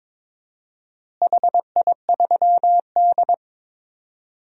Morse code, heard as a keyed tone, about 21 wpm, 700 Hz. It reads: HI3D
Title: DXCC-100 Call Signs - 4 Characters - Repeated 2x - 22wpm